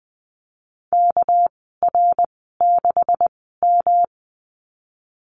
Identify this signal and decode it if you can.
Morse code, heard as a keyed tone, about 20 wpm, spaced as for 15 wpm, 700 Hz. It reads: KR6M
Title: 213 Lesson 34g - Number 8 - Call Signs 20wpm